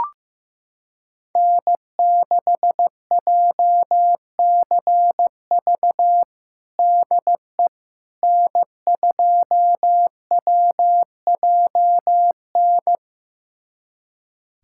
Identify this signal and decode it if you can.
Morse code, heard as a keyed tone, about 15 wpm, 700 Hz. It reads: N6JCV DE N2WJN
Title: USA Callsign DE Callsign - 5 Characters - 15wpm